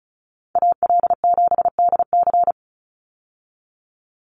Morse code, heard as a keyed tone, about 34 wpm, 700 Hz. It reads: AL7DC
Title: USA Call Signs 35wpm